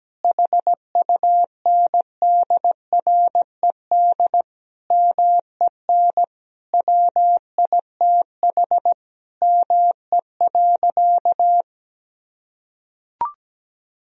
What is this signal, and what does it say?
Morse code, heard as a keyed tone, about 17 wpm, 700 Hz. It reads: HUNDRED MEN WITH ME.
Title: Sentences from Top 1000 Words - Review Part 2 - 17wpm